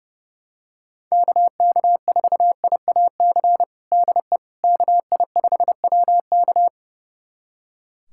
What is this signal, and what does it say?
Morse code, heard as a keyed tone, about 32 wpm, 700 Hz. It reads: KK4IAC DE KI5WK
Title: USA Callsign DE Callsign 30wpm